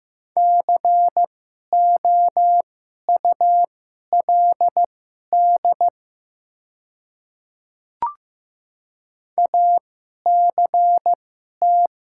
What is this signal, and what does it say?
Morse code, heard as a keyed tone, about 15 wpm, spaced as for 11 wpm, 700 Hz. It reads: COULD ACT
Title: 2x Character Spacing - Top 300 Words - 15wpm